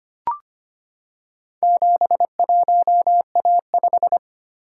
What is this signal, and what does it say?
Morse code, heard as a keyed tone, about 25 wpm, 700 Hz. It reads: 71A5
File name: practice / 25wpm / Four Letter-Number Combinations - 25wpm